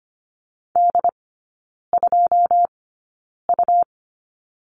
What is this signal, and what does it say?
Morse code, heard as a keyed tone, about 25 wpm, spaced as for 9 wpm, 700 Hz. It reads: D2U
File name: Lesson 17e - Letter u - Three Characters 25wpm